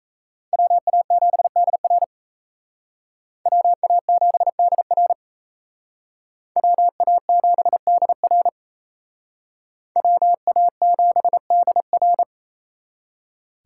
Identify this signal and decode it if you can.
Morse code, heard as a keyed tone, about 35 wpm, 700 Hz. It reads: WA7DR WA7DR WA7DR WA7DR
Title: Speed Racing - USA 2x2 Call Signs 28wpm